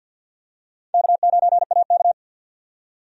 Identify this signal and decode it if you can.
Morse code, heard as a keyed tone, about 49 wpm, 700 Hz. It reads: K9AK
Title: USA Call Signs - 4 Characters - 50wpm